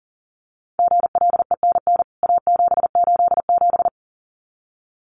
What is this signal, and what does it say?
Morse code, heard as a keyed tone, about 40 wpm, 700 Hz. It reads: GLENN A787
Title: CWT Exchange 40wpm